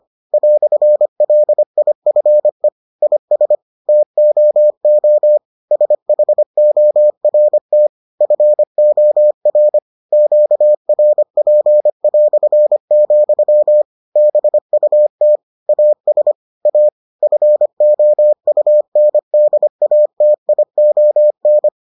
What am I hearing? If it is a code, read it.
Morse code, heard as a keyed tone, about 25 wpm, 600 Hz. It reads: "LIFE IS TOO SHORT FOR QRP", BUT AS A FOUNDATION